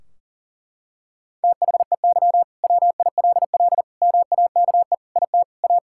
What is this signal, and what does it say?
Morse code, heard as a keyed tone, about 40 wpm, 700 Hz. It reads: THEY WILL MAKE IT A
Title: Sentences from Top 1600 Words 40wpm